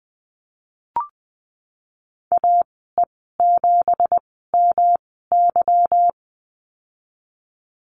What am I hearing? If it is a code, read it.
Morse code, heard as a keyed tone, about 20 wpm, spaced as for 15 wpm, 700 Hz. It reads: AE7MY
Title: 167 Lesson 27g - Number 7 - Call Signs 20wpm